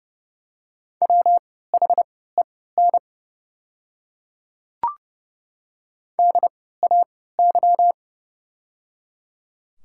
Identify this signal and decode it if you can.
Morse code, heard as a keyed tone, about 30 wpm, spaced as for 17 wpm, 700 Hz. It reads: WHEN DAY
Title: 3x Character Spacing - Top 100 Words - 30wpm